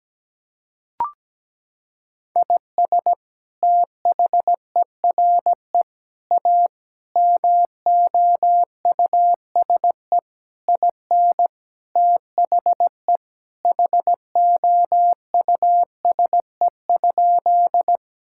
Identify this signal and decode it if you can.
Morse code, heard as a keyed tone, about 17 wpm, 700 Hz. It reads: IS THERE A MOUSE IN THE HOUSE?
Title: Sentences from Top 1000 Words - Review Part 5 - 17wpm